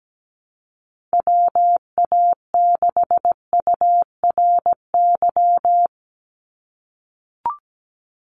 Morse code, heard as a keyed tone, about 17 wpm, 700 Hz. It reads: WA6URY
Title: Call Signs - 17wpm